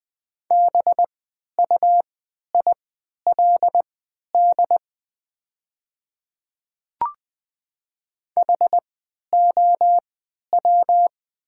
Morse code, heard as a keyed tone, about 20 wpm, spaced as for 11 wpm, 700 Hz. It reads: BUILD HOW
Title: 3x Character Spacing - Top 200 Words - 20wpm